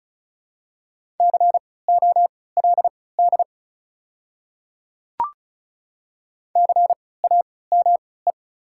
Morse code, heard as a keyed tone, about 36 wpm, spaced as for 20 wpm, 700 Hz. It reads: COLD CAME
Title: 3x Character Spacing - Top 500 Words - 35wpm